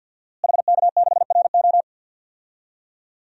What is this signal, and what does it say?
Morse code, heard as a keyed tone, about 52 wpm, 700 Hz. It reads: HK6RO
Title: DXCC-100 Call Signs - 5 Characters - Repeated 3x - 50wpm